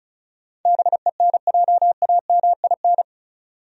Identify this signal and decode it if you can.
Morse code, heard as a keyed tone, about 36 wpm, 700 Hz. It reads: BENJAMIN
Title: USA Common Names - 35wpm